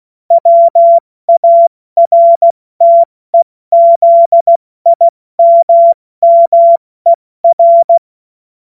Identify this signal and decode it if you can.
Morse code, heard as a keyed tone, about 16 wpm, 680 Hz. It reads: WARTEZIMMER
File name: Morsekurs / Witze / Witz 25. Juli 2023 70 F80